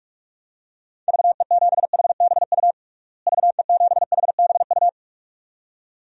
Morse code, heard as a keyed tone, about 42 wpm, 700 Hz. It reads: VE7HBU VE7HBU
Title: DXCC-100 Call Signs - 6 Characters - Repeated 2x - 45wpm